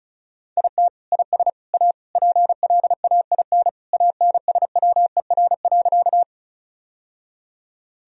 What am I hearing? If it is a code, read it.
Morse code, heard as a keyed tone, about 36 wpm, 700 Hz. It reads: IT IS A PLAIN ANSWER.
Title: Sentences from Top 1000 Words - Review Part 2 - 35wpm